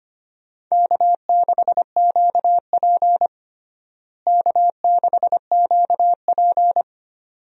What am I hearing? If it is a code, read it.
Morse code, heard as a keyed tone, about 25 wpm, 700 Hz. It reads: K6QP K6QP
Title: USA Call Signs - 4 Characters - Repeated 2x - 25wpm